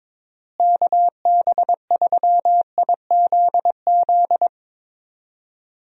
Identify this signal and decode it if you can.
Morse code, heard as a keyed tone, about 22 wpm, 700 Hz. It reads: KB3IZZ
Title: USA Call Signs - 6 Characters - Repeated 2x - 22wpm